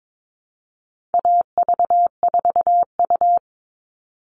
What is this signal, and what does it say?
Morse code, heard as a keyed tone, about 22 wpm, 700 Hz. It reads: AV4U
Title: All Cut Numbers - 4-Digit 22wpm